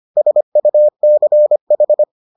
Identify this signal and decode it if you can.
Morse code, heard as a keyed tone, about 25 wpm, 600 Hz. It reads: SUCH